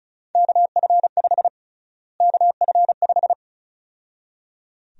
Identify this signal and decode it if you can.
Morse code, heard as a keyed tone, about 36 wpm, 700 Hz. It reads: KF5 KF5
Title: USA Call Sign Prefixes - Repeated 2x - 35wpm